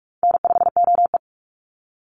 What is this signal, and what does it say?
Morse code, heard as a keyed tone, about 44 wpm, 700 Hz. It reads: N5OE
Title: Call Signs - 45wpm